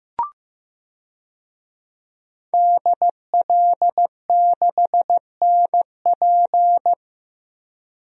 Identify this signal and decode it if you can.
Morse code, heard as a keyed tone, about 15 wpm, 700 Hz. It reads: DL6NP
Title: DXCC-100 Call Signs - 5 Characters - 15wpm